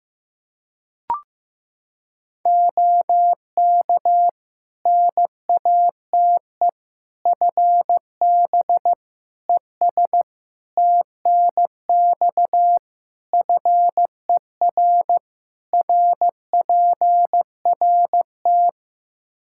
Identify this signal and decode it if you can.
Morse code, heard as a keyed tone, about 15 wpm, 700 Hz. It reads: OK NATE FB ES TNX FER RPRT
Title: Practice QSO - Exchange 2 - Info - 15wpm